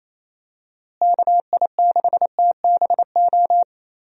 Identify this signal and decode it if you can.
Morse code, heard as a keyed tone, about 27 wpm, 700 Hz. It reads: KI6TBO
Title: USA Call Signs - 6 Characters - Repeated 3x - 28wpm